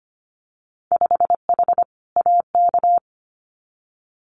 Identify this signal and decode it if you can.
Morse code, heard as a keyed tone, about 25 wpm, 700 Hz. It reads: 5H AK
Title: Winter Field Day Exchanges Encore 25wpm